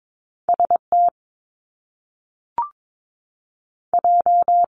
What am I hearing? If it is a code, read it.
Morse code, heard as a keyed tone, about 22 wpm, 700 Hz. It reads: S T J
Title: Top Two Letter Combinations in Words 22wpm